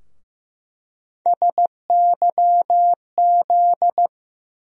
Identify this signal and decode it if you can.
Morse code, heard as a keyed tone, about 15 wpm, 700 Hz. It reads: SYZ
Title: USA Call Sign Postfixes - 3 Letters 15wpm